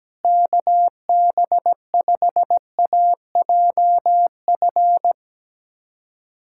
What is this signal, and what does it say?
Morse code, heard as a keyed tone, about 17 wpm, 700 Hz. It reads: KB5AJF
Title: USA Callsign DE Callsign 17wpm